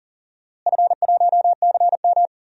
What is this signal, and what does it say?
Morse code, heard as a keyed tone, about 40 wpm, 700 Hz. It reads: F1CM
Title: DXCC-100 Call Signs - 4 Characters - Repeated 2x - 40wpm